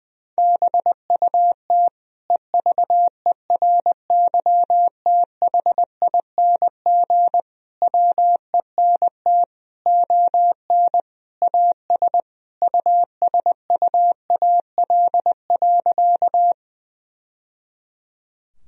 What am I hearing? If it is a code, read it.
Morse code, heard as a keyed tone, about 21 wpm, 700 Hz. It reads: BUT EVERYTHING WENT ON AS USUAL.
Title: Sentences from Top 1000 Words - Review Part 2 - 20wpm